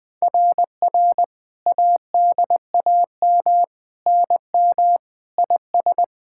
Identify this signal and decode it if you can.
Morse code, heard as a keyed tone, about 21 wpm, 700 Hz. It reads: RR ADAM NM IS
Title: SOTA QSO Practice 20wpm